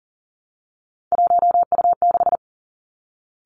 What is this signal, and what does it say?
Morse code, heard as a keyed tone, about 40 wpm, 700 Hz. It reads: 1U6
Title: Three Challenging Letter-Number Combinations 40wpm